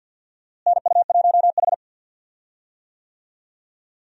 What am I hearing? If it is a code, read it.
Morse code, heard as a keyed tone, about 52 wpm, 700 Hz. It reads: NU1H
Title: USA Call Signs - 4 Characters - Repeated 3x - 50wpm